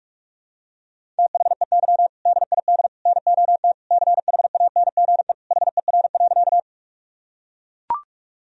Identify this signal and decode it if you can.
Morse code, heard as a keyed tone, about 41 wpm, 700 Hz. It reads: THEY DID NOT CHANGE HER.
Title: Sentences from Top 1000 Words - Review Part 1 - 45wpm